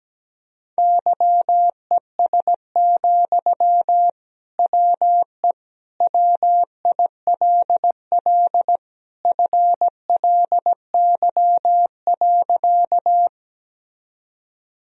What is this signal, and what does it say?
Morse code, heard as a keyed tone, about 17 wpm, 700 Hz. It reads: YES, WE WILL FLY.
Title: Sentences from Top 1000 Words - Review Part 2 - 17wpm